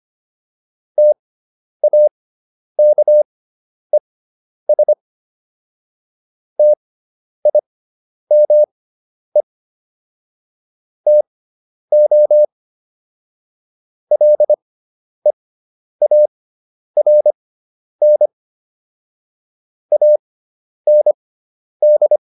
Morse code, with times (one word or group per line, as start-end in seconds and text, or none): 1.0-4.9 TAKES
6.6-9.4 TIME
11.1-12.4 TO
14.1-18.3 LEARN
19.9-22.2 AND